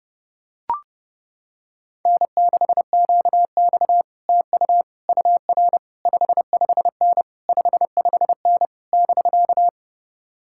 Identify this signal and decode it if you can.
Morse code, heard as a keyed tone, about 30 wpm, 700 Hz. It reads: N6QX TU UR 55N 55N <BK>